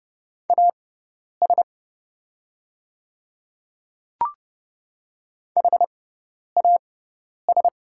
Morse code, heard as a keyed tone, about 30 wpm, spaced as for 10 wpm, 700 Hz. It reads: AS HAS
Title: 6x Character Spacing - Top 200 Words - 30wpm